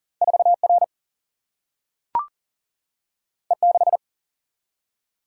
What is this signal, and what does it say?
Morse code, heard as a keyed tone, about 39 wpm, 700 Hz. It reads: 4 R E 6
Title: Single-Letter Single-Number 40wpm